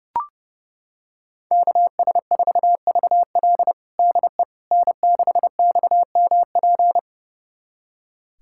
Morse code, heard as a keyed tone, about 30 wpm, 700 Hz. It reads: KS4VL DE N6XMP
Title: USA Callsign DE Callsign - 5 Characters - 30wpm